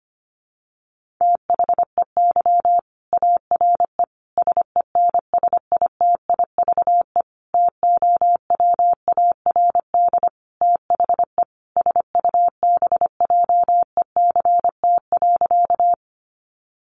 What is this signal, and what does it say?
Morse code, heard as a keyed tone, about 25 wpm, 700 Hz. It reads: THEY ARE SENSITIVE TOWARD THE SUBJECT.
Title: Sentences from Top 1400 Words 25wpm